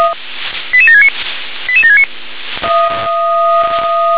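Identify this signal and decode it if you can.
Five-tone selective call (ZVEI), heard as quick runs of five tones, tones of about 70 ms. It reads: 88769 88769
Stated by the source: ZVEI five-tone selective call received on SDR.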